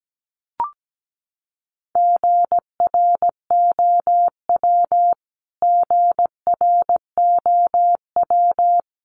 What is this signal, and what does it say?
Morse code, heard as a keyed tone, about 17 wpm, 700 Hz. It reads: GROW GROW
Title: Top 500 Words Repeated 17wpm